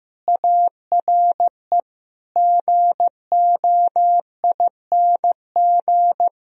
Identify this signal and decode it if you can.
Morse code, heard as a keyed tone, about 16 wpm, 700 Hz. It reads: ARE GOING
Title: Speed-Racing - Sets of 4 Words 15wpm